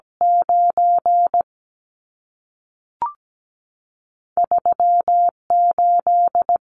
Morse code, heard as a keyed tone, about 17 wpm, 700 Hz. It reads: 9 3 8
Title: Numbers - 2-Digit 17wpm